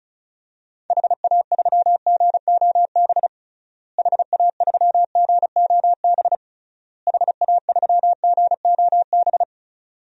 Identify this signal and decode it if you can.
Morse code, heard as a keyed tone, about 36 wpm, 700 Hz. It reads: HA3GOB HA3GOB HA3GOB
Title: DXCC-100 Call Signs - 6 Characters - Repeated 3x - 35wpm